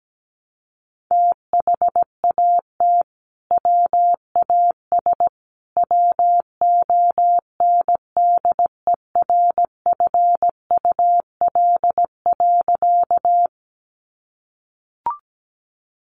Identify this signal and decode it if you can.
Morse code, heard as a keyed tone, about 17 wpm, 700 Hz. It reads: THAT WAS WONDERFUL.
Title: Sentences from Top 1100 Words 17wpm